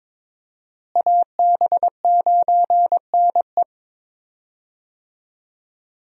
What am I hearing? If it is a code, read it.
Morse code, heard as a keyed tone, about 22 wpm, 700 Hz. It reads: AB9NE
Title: USA Call Signs - 5 Characters - Repeated 3x - 22wpm